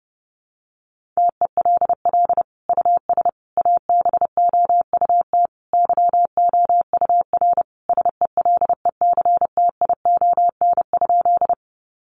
Morse code, heard as a keyed tone, about 30 wpm, 700 Hz. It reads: TELL US ABOUT YOUR SELECTION?
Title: Sentences from Top 1400 Words 30wpm